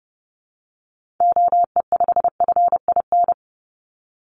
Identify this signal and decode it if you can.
Morse code, heard as a keyed tone, about 30 wpm, 700 Hz. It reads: OE5FIN
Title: Call Signs - 30wpm